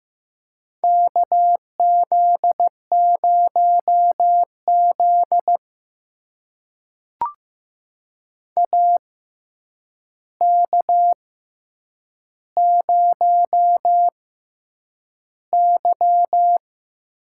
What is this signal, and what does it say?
Morse code, heard as a keyed tone, about 15 wpm, spaced as for 8 wpm, 700 Hz. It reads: KZ0Z AK0Y
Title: USA Call Signs - 4 Characters - 6x Character Spacing - 15wpm